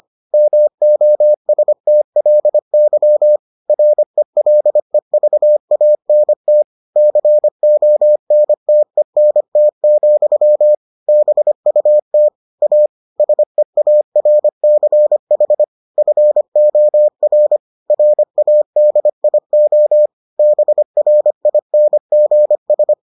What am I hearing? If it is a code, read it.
Morse code, heard as a keyed tone, about 25 wpm, 600 Hz. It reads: MOSTLY RELEVANT CONTENT, BUT A SEARCH FOR RADIO BRINGS